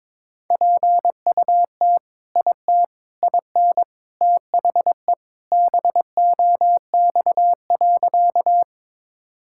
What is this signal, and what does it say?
Morse code, heard as a keyed tone, about 21 wpm, 700 Hz. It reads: PUT IT IN THE BOX.
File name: practice / 22wpm / Sentences with 2-Word Phrases - 22wpm